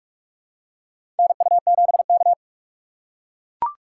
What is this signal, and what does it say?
Morse code, heard as a keyed tone, about 44 wpm, 690 Hz. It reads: NU7K
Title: USA Call Signs - 4 Characters - 4x Character Spacing - 45wpm